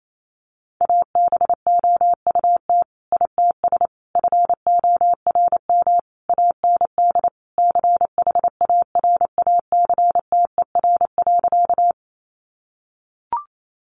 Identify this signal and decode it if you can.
Morse code, heard as a keyed tone, about 28 wpm, 700 Hz. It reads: ABOUT ITS FORM AND CHARACTER.
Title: Sentences from Top 1200 Words 28wpm